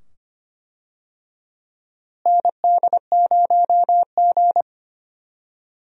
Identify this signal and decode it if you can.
Morse code, heard as a keyed tone, about 25 wpm, 700 Hz. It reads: ND0G
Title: USA Call Signs - 4 Characters - Repeated 2x - 25wpm